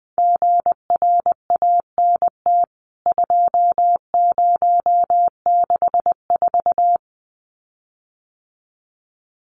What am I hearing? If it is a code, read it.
Morse code, heard as a keyed tone, about 19 wpm, 700 Hz. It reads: GRANT 2064
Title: CWT Exchange 20wpm